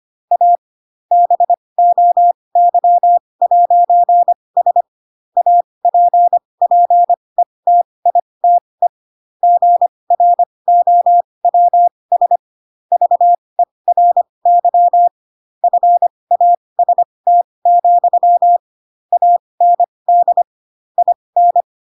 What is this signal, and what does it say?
Morse code, heard as a keyed tone, about 25 wpm, spaced as for 20 wpm, 700 Hz. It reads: A BOY'S APPETITE GROWS VERY FAST, AND IN
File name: Pinocchio / Pinocchio ch5 15 25 ik1hge